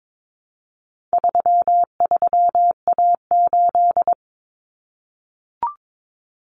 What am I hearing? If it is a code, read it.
Morse code, heard as a keyed tone, about 22 wpm, 700 Hz. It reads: 33A8
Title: Common Cut Numbers - 4-Digit 22wpm